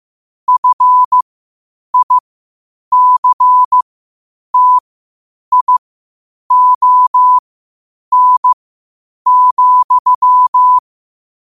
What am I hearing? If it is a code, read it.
Morse code, heard as a keyed tone, about 15 wpm, spaced as for 8 wpm, 1000 Hz. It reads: FICTION,
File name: qotdfiles / 8wpm